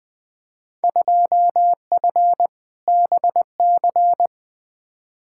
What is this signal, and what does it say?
Morse code, heard as a keyed tone, about 20 wpm, 700 Hz. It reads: 2F BC
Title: Field Day Exchanges Encore - 20wpm